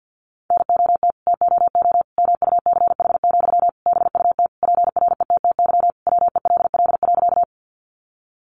Morse code, heard as a keyed tone, about 50 wpm, 700 Hz. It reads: NOT TOO MUCH, BUT PRETTY WELL.
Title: Sentences from Top 1000 Words 50wpm